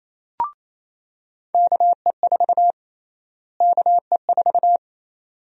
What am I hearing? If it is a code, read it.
Morse code, heard as a keyed tone, about 28 wpm, 700 Hz. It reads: KE4 KE4
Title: USA Call Sign Prefixes - Repeated 2x - 28wpm